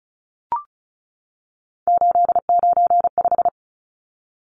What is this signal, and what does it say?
Morse code, heard as a keyed tone, about 34 wpm, 700 Hz. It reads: 895
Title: Numbers - 3-Digit 35wpm